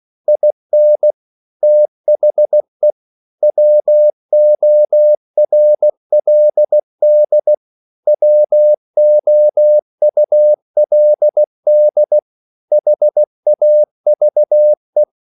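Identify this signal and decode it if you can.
Morse code, heard as a keyed tone, about 16 wpm, 600 Hz. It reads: IN THE WORLD WOULD HAVE
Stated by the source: English quotations sent in CW as practice at 16 words per minute.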